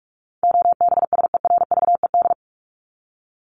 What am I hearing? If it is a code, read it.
Morse code, heard as a keyed tone, about 44 wpm, 700 Hz. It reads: OBSERVED
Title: ED Suffix - Top 100 Words with Base Words 45wpm